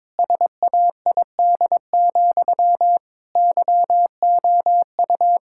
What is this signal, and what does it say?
Morse code, heard as a keyed tone, about 21 wpm, 700 Hz. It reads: SAID, YOU
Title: Sentences from Top 1600 Words 22wpm